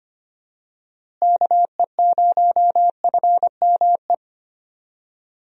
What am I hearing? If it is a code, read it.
Morse code, heard as a keyed tone, about 25 wpm, 700 Hz. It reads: KE0FME
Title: USA Call Signs - 6 Characters - Repeated 3x - 25wpm